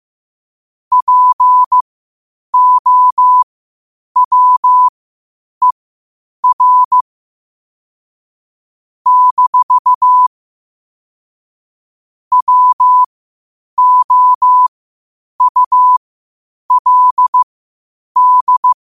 Morse code, heard as a keyed tone, about 15 wpm, spaced as for 8 wpm, 1000 Hz. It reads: POWER - WOULD
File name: qotdfiles / 8wpm